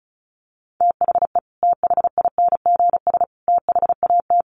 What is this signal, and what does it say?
Morse code, heard as a keyed tone, about 34 wpm, 700 Hz. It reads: THE THINGS THAT
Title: Sets of 3 Words 35wpm